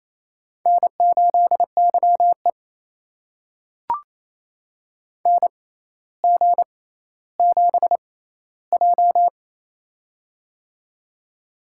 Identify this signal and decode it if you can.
Morse code, heard as a keyed tone, about 27 wpm, spaced as for 14 wpm, 700 Hz. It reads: N8YE NG7J
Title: USA Call Signs - 4 Characters - 6x Character Spacing - 28wpm